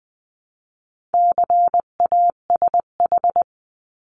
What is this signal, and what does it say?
Morse code, heard as a keyed tone, about 20 wpm, 700 Hz. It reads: CASH
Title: 259 Lesson 42d - Words 18-20wpm